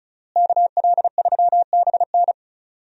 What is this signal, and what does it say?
Morse code, heard as a keyed tone, about 36 wpm, 700 Hz. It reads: KL3BN
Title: USA Call Signs - 5 Characters - Repeated 3x - 35wpm